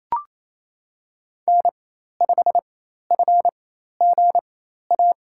Morse code, heard as a keyed tone, about 28 wpm, spaced as for 13 wpm, 700 Hz. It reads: N5FGA
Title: USA Call Signs - 5 Characters - 4x Character Spacing - 28wpm